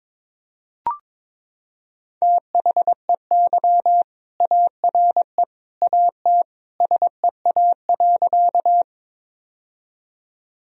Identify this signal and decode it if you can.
Morse code, heard as a keyed tone, about 22 wpm, 700 Hz. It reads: THEY ARE AT SEA.
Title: Sentences from Top 1000 Words - Review Part 1 - 22wpm